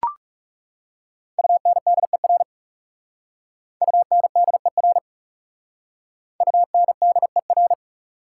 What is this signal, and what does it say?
Morse code, heard as a keyed tone, about 38 wpm, 700 Hz. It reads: UNDER UNDER UNDER